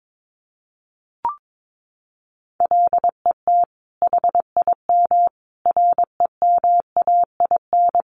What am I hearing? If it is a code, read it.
Morse code, heard as a keyed tone, about 22 wpm, 700 Hz. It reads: LET HIM REMAIN
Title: Sentences from Top 1200 Words 22wpm